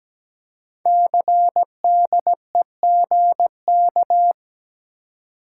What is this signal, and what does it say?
Morse code, heard as a keyed tone, about 17 wpm, 700 Hz. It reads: CDEGK
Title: Code Groups - 5 Letters - 17wpm